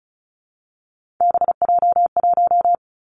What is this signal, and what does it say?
Morse code, heard as a keyed tone, about 36 wpm, 700 Hz. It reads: BJ1
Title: Three Challenging Letter-Number Combinations 35wpm